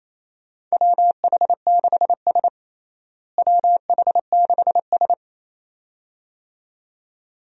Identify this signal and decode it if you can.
Morse code, heard as a keyed tone, about 28 wpm, 700 Hz. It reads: WH6S WH6S
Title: USA Call Signs - 4 Characters - Repeated 2x - 28wpm